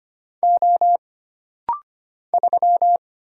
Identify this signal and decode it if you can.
Morse code, heard as a keyed tone, about 25 wpm, 700 Hz. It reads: O 3
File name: Single Character (Letter:Number:Punctuation:Prosign) - Rapid-Fire - 25wpm